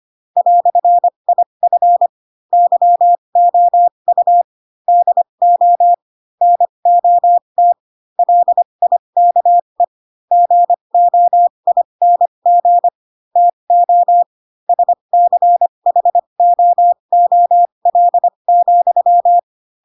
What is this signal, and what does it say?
Morse code, heard as a keyed tone, about 25 wpm, spaced as for 22 wpm, 700 Hz. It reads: "IF YOU DO NOT LIKE GOING TO SCHOOL,
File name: Pinocchio / Pinocchio ch4 18 25 ik1hge